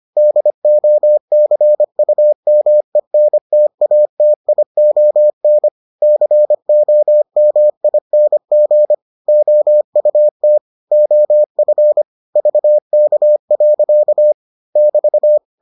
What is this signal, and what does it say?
Morse code, heard as a keyed tone, about 25 wpm, 600 Hz. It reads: DOCUMENTATION COMING OUT OF VK. <BT>